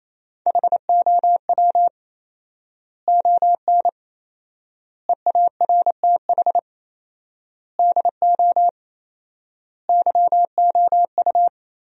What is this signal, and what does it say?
Morse code, heard as a keyed tone, about 28 wpm, 700 Hz. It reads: HOW ON EARTH DO YOU
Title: 4x Word-Spacing - Sentences from Top 200 Words - 28wpm